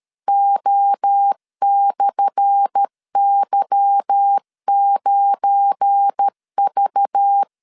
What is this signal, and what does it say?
Morse code, heard as a keyed tone, about 13 wpm, 800 Hz. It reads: O/Y9V
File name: LEZIONI / 10 SIGNS / 10A SLASH 12wpm 2314